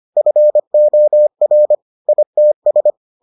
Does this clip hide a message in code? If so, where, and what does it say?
Yes: Morse code from 0.2 to 2.9 s, heard as a keyed tone, about 25 wpm, 600 Hz. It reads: FOR ITS